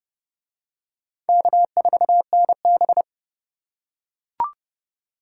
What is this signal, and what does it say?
Morse code, heard as a keyed tone, about 28 wpm, 700 Hz. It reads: K4NB